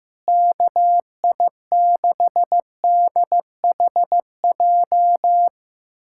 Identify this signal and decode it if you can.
Morse code, heard as a keyed tone, about 15 wpm, 700 Hz. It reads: KI6DHJ
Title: USA Callsign DE Callsign 15wpm